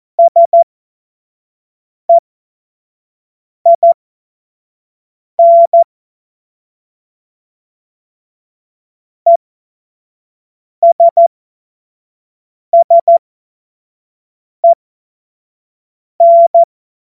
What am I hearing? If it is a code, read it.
Morse code, heard as a keyed tone, about 14 wpm, spaced as for 5 wpm, 680 Hz. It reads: SEIN ESSEN
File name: Morsekurs / Witze / Witz 06 Februar 2024 25 F70